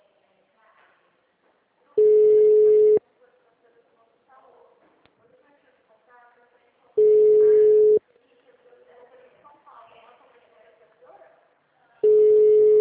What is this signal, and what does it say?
Ringback tone (Europe, 425 Hz), with speech.